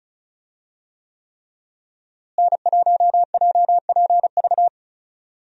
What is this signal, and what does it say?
Morse code, heard as a keyed tone, about 36 wpm, 700 Hz. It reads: N1JPV
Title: USA Call Signs - 5 Characters - Repeated 3x - 35wpm